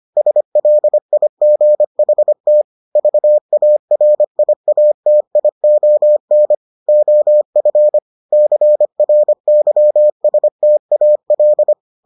Morse code, heard as a keyed tone, about 25 wpm, 600 Hz. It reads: SLIGHT VARIATION OF CRYSTAL